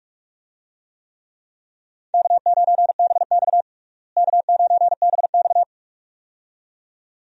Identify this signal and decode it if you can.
Morse code, heard as a keyed tone, about 46 wpm, 700 Hz. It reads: K9BX K9BX
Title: USA Call Signs - 4 Characters - Repeated 2x - 45wpm